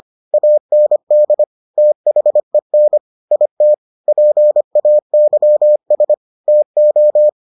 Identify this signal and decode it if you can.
Morse code, heard as a keyed tone, about 25 wpm, 600 Hz. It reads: AND THEN IT PAYS TO